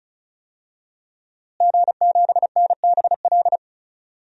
Morse code, heard as a keyed tone, about 36 wpm, 700 Hz. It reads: G7NBL
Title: DXCC-100 Call Signs - 5 Characters - Repeated 2x - 35wpm